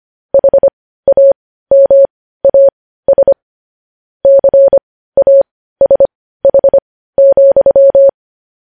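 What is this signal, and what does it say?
Morse code, heard as a keyed tone, about 25 wpm, spaced as for 15 wpm, 550 Hz. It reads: HAMAS CASH,
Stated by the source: Morse code news headlines sent at 15 wpm.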